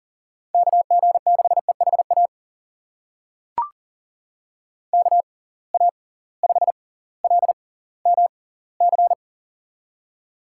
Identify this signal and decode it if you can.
Morse code, heard as a keyed tone, about 40 wpm, spaced as for 20 wpm, 700 Hz. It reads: KG6EHA KA5LMC
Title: USA Call Signs - 6 Characters - 6x Character Spacing - 40wpm